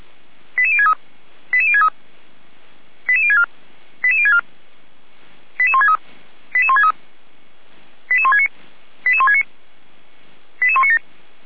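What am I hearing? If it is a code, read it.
Five-tone selective call (ZVEI), heard as quick runs of five tones, tones of about 70 ms. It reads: 88063 88063 88064 88064 88163 88163 88169 88169 88197
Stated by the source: ZVEI five-tone call received by SDR.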